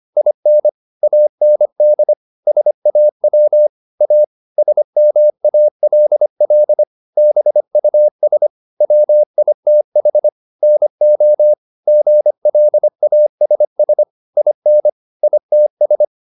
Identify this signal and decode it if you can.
Morse code, heard as a keyed tone, about 25 wpm, 600 Hz. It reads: IN AND SAW A SMALL BUS WITH NO GLASS IN ITS